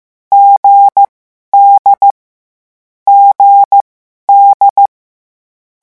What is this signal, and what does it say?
Morse code, heard as a keyed tone, about 15 wpm, spaced as for 11 wpm, 800 Hz. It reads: GD GD
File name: morse mp3 / lecon4-vitesse 14